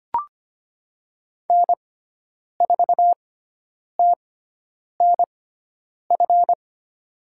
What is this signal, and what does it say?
Morse code, heard as a keyed tone, about 25 wpm, spaced as for 9 wpm, 700 Hz. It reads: N4TNF
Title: USA Call Signs - 5 Characters - 6x Character Spacing - 25wpm